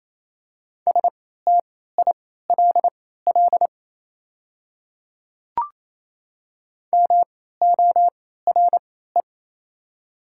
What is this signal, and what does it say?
Morse code, heard as a keyed tone, about 28 wpm, spaced as for 16 wpm, 700 Hz. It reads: STILL MORE
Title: 3x Character Spacing - Top 300 Words - 28wpm